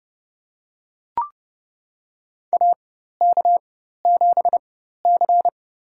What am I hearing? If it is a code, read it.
Morse code, heard as a keyed tone, about 29 wpm, spaced as for 14 wpm, 700 Hz. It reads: AK7C
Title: USA Call Signs - 4 Characters - 4x Character Spacing - 30wpm